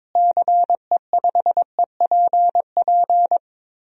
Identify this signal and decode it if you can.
Morse code, heard as a keyed tone, about 22 wpm, 700 Hz. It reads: CE5EPP
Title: DXCC-100 Call Signs - 6 Characters - Repeated 3x - 22wpm